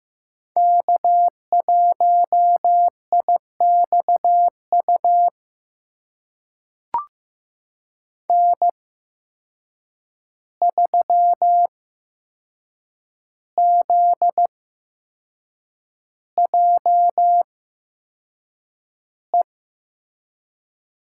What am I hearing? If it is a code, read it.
Morse code, heard as a keyed tone, about 15 wpm, spaced as for 6 wpm, 700 Hz. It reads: K1IXU N3ZJE
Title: USA Call Signs - 5 Characters - 8x Character Spacing - 15wpm